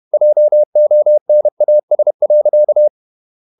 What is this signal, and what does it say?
Morse code, heard as a keyed tone, about 31 wpm, 600 Hz. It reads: JONAS.
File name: Ferie QRQ 31wpm